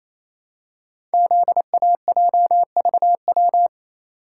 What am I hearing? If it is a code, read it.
Morse code, heard as a keyed tone, about 27 wpm, 700 Hz. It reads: ZAJVW
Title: Code Groups - 5 Letters - 28wpm